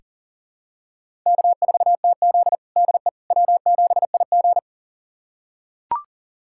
Morse code, heard as a keyed tone, about 40 wpm, 700 Hz. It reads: K4TZ DE W7IG
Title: USA Callsign DE Callsign - 4 Characters - 40wpm